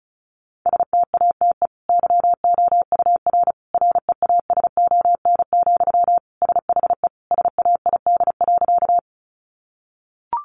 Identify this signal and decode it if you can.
Morse code, heard as a keyed tone, about 34 wpm, 700 Hz. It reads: STATE YOUR REASON, SHE SAID.
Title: Sentences from Top 700 Words 35wpm